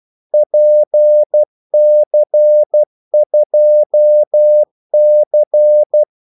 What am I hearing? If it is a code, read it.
Morse code, heard as a keyed tone, about 12 wpm, 600 Hz. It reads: PC2C
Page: 2 ekstra ordmellemrum mellem kaldesignalerne